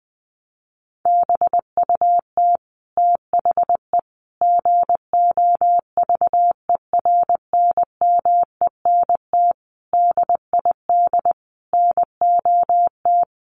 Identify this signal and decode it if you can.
Morse code, heard as a keyed tone, about 20 wpm, 700 Hz. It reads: BUT THE GOVERNMENT DID NOT
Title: Sentences from Top 1400 Words 20wpm